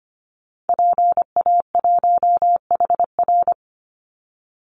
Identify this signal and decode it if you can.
Morse code, heard as a keyed tone, about 25 wpm, 700 Hz. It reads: PA1HR
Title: Call Signs Repeated 2x - 25wpm